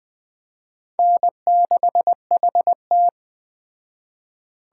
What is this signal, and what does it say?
Morse code, heard as a keyed tone, about 21 wpm, 700 Hz. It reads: N6HT